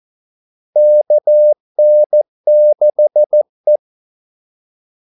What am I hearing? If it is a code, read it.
Morse code, heard as a keyed tone, about 14 wpm, 600 Hz. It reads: KN6E
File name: Callsigns-14wpm